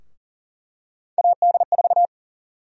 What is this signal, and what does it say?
Morse code, heard as a keyed tone, about 40 wpm, 700 Hz. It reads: AD4
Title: USA Call Sign Prefixes - Repeated 2x - 40wpm